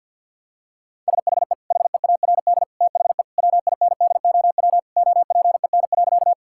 Morse code, heard as a keyed tone, about 50 wpm, 700 Hz. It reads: SHE HEARD THE WINDOW OPEN.
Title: Sentences from Top 1600 Words 50wpm